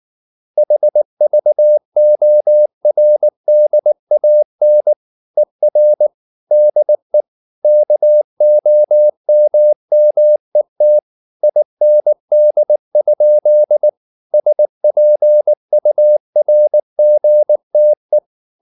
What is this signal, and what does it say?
Morse code, heard as a keyed tone, about 19 wpm, 600 Hz. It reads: HVORDAN ER DE KOMMET IND? SPURGTE
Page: Ferie 19 wpm